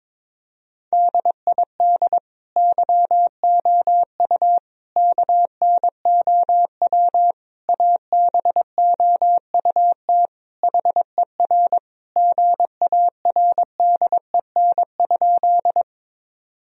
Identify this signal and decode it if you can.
Morse code, heard as a keyed tone, about 22 wpm, 700 Hz. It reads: DID YOU KNOW ABOUT HER GARDEN?
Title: Sentences from Top 1000 Words - Review Part 4 - 22wpm